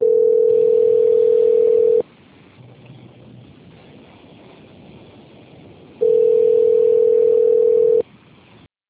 Ringback tone (North America), 440 + 480 Hz.
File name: out- - - - - .62